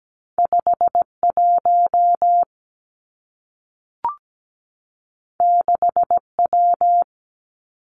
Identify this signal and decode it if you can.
Morse code, heard as a keyed tone, about 17 wpm, 700 Hz. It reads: 5 1 6 W
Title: Two Challenging Letter-Number Combinations 17wpm